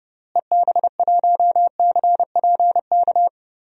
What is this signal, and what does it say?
Morse code, heard as a keyed tone, about 32 wpm, 700 Hz. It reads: EB1CPK
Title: DXCC-100 Call Signs - 6 Characters - Repeated 3x - 30wpm